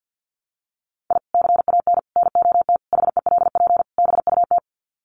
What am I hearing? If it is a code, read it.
Morse code, heard as a keyed tone, about 50 wpm, 700 Hz. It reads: I CAN NOT HELP BUT
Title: Sets of 5 Words Encore2 50wpm